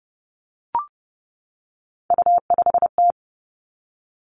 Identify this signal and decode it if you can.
Morse code, heard as a keyed tone, about 30 wpm, 700 Hz. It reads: U5T
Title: Three Challenging Letter-Number Combinations 30wpm